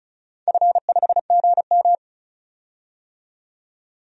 Morse code, heard as a keyed tone, about 36 wpm, 700 Hz. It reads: F5GM
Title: DXCC-100 Call Signs - 4 Characters - 35wpm